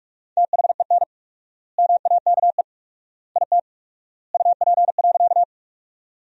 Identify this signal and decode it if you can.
Morse code, heard as a keyed tone, about 42 wpm, 700 Hz. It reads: THEN MAKE IT UP.
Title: 4x Word-Spacing - Sentences from Top 100 Words - 45wpm